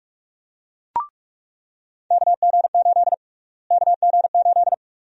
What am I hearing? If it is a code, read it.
Morse code, heard as a keyed tone, about 46 wpm, 700 Hz. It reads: KG8 KG8
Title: USA Call Sign Prefixes - Repeated 2x - 45wpm